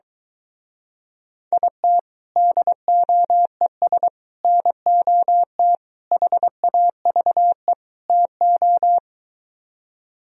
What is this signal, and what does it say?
Morse code, heard as a keyed tone, about 23 wpm, 700 Hz. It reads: IT DOES NOT HAVE TO